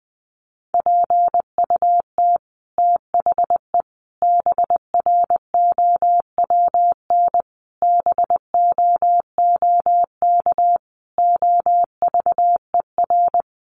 Morse code, heard as a keyed tone, about 20 wpm, 700 Hz. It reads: PUT THE BROWN BOOK OVER
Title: Sentences from Top 1200 Words 20wpm